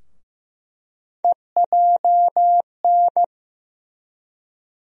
Morse code, heard as a keyed tone, about 15 wpm, 700 Hz. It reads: EJN
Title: USA Call Sign Postfixes - 3 Letters 15wpm